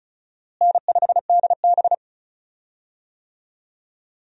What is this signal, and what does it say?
Morse code, heard as a keyed tone, about 36 wpm, 700 Hz. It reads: N5DB
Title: USA Call Signs - 4 Characters - Repeated 3x - 35wpm